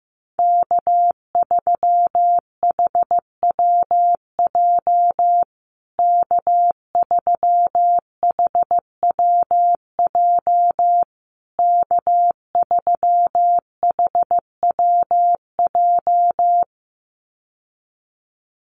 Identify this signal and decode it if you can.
Morse code, heard as a keyed tone, about 15 wpm, 700 Hz. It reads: K3HWJ K3HWJ K3HWJ
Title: USA Call Signs Repeated 3x 15wpm